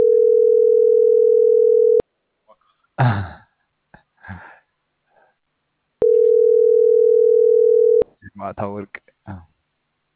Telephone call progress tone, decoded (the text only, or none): ringback tone